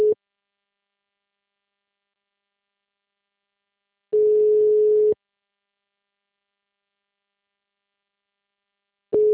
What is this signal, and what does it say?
Ringback tone (Europe, 425 Hz).